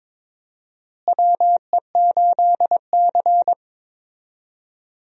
Morse code, heard as a keyed tone, about 21 wpm, 700 Hz. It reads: WE8C
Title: USA Call Signs - 4 Characters - Repeated 2x - 22wpm